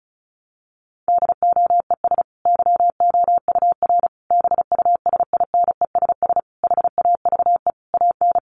Morse code, heard as a keyed tone, about 34 wpm, 700 Hz. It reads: DOES YOUR BUSINESS HAVE AN
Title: Sentences from Top 1200 Words 35wpm